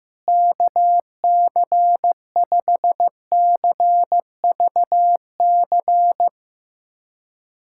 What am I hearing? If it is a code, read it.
Morse code, heard as a keyed tone, about 16 wpm, 700 Hz. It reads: KC5CVC
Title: USA Callsign DE Callsign 15wpm